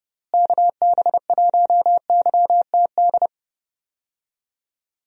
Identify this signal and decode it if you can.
Morse code, heard as a keyed tone, about 30 wpm, 700 Hz. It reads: KB1YTD
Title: USA Call Signs - 6 Characters - Repeated 2x - 30wpm